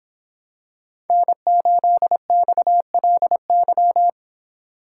Morse code, heard as a keyed tone, about 25 wpm, 700 Hz. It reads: N8XLY